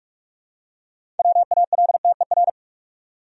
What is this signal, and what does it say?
Morse code, heard as a keyed tone, about 45 wpm, 700 Hz. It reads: WALTER
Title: USA Common Names - 45wpm